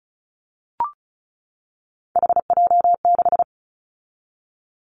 Morse code, heard as a keyed tone, about 36 wpm, 700 Hz. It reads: HJ6
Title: Callsign Prefixes for Top 100 DXCC - 35wpm